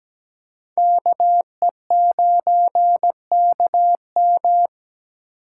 Morse code, heard as a keyed tone, about 21 wpm, 700 Hz. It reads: KE9KM
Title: Speed Racing - USA 2x2 Call Signs 17wpm